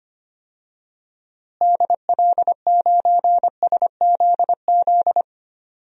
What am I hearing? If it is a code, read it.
Morse code, heard as a keyed tone, about 25 wpm, 700 Hz. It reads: DL9SZZ
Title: DXCC-100 Call Signs - 6 Characters - Repeated 2x - 25wpm